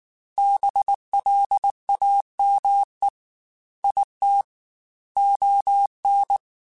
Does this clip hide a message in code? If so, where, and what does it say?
Yes: Morse code from 0.4 to 6.4 s, heard as a keyed tone, about 19 wpm, 800 Hz. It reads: BLAME IT ON